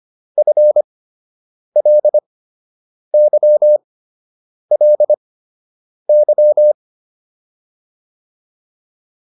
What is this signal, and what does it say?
Morse code, heard as a keyed tone, about 25 wpm, spaced as for 8 wpm, 600 Hz. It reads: FLYLY